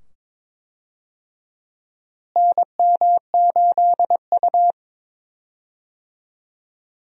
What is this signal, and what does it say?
Morse code, heard as a keyed tone, about 21 wpm, 700 Hz. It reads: NM8U
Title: USA Call Signs - 4 Characters - Repeated 2x - 22wpm